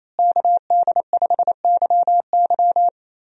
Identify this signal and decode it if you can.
Morse code, heard as a keyed tone, about 28 wpm, 700 Hz. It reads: KD5YY
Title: USA Call Signs - 5 Characters - Repeated 2x - 28wpm